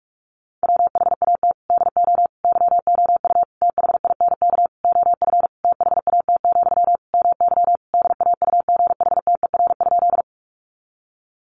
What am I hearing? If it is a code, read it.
Morse code, heard as a keyed tone, about 44 wpm, 700 Hz. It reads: WHAT DO YOU THINK OF THAT, MY DAUGHTER?
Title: Sentences from Top 1500 Words 45wpm